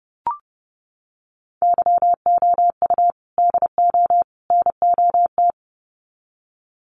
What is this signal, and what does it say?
Morse code, heard as a keyed tone, about 30 wpm, 700 Hz. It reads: YOU DO NOT
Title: Sets of 3 Words 30wpm